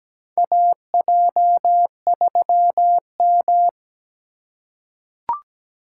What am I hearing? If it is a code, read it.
Morse code, heard as a keyed tone, about 17 wpm, 700 Hz. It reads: AJ3M
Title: USA Call Signs - 4 Characters - 2x Character Spacing - 17wpm